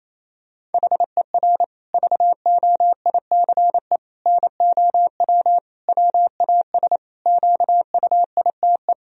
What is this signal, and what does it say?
Morse code, heard as a keyed tone, about 28 wpm, 700 Hz. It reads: HER VOICE NOW WAS QUITE
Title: Sentences from Top 1000 Words - Review Part 3 - 28wpm